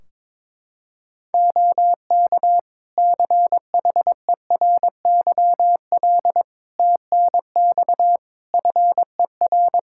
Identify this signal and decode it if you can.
Morse code, heard as a keyed tone, about 22 wpm, 700 Hz. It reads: OK CHERYL TNX FER
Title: Practice QSO - Exchange 4 - Ending - 22wpm